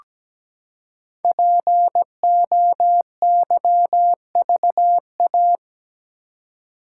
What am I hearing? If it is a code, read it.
Morse code, heard as a keyed tone, about 17 wpm, 700 Hz. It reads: POYVA
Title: Code Groups - 5 Letters - 17wpm